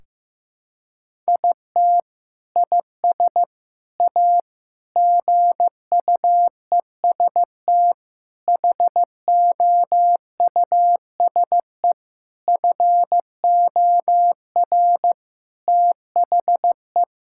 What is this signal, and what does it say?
Morse code, heard as a keyed tone, about 15 wpm, 700 Hz. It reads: IT IS A GUEST HOUSE FOR THE
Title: Sentences from Top 1600 Words 15wpm